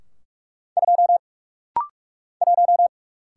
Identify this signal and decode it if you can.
Morse code, heard as a keyed tone, about 45 wpm, 700 Hz. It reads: 2 1
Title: Single Letter-Number - Rapid-Fire 45wpm